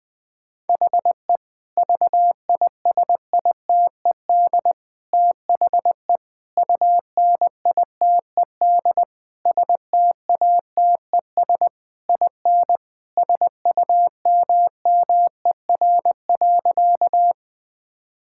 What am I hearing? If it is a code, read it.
Morse code, heard as a keyed tone, about 20 wpm, 700 Hz. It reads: HE VISITED THE UNITED STATES IN SUMMER.
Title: Sentences with 4-Word Phrases - 20wpm